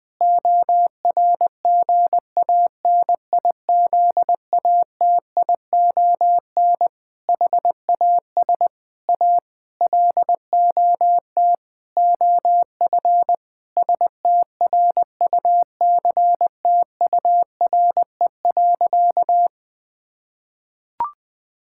Morse code, heard as a keyed tone, about 20 wpm, 700 Hz. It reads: ORGANIZATION HAS A LOT OF STRUCTURE.
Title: Sentences from Top 1000 Words - Review Part 6 - 20wpm